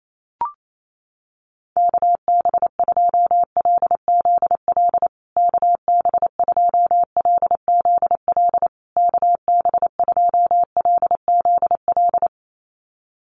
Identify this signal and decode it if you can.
Morse code, heard as a keyed tone, about 28 wpm, 700 Hz. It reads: KB2LZL KB2LZL KB2LZL
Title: USA Call Signs Repeated 3x 28wpm